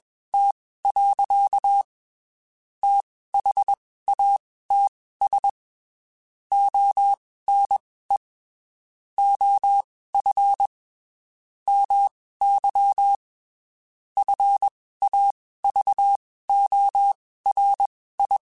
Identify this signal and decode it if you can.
Morse code, heard as a keyed tone, about 21 wpm, spaced as for 15 wpm, 800 Hz. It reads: T. THATS ONE OF MY FAVORI
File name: QSOs / QSOs-mp3s / WD0FBT 1@20